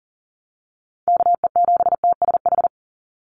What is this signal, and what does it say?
Morse code, heard as a keyed tone, about 40 wpm, 700 Hz. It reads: KE7TSH
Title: USA Call Signs 40wpm